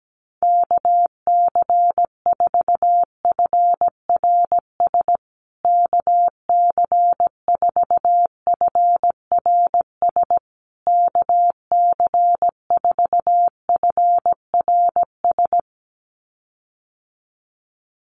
Morse code, heard as a keyed tone, about 17 wpm, 700 Hz. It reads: KC4FRS KC4FRS KC4FRS
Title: USA Call Signs Repeated 3x 17wpm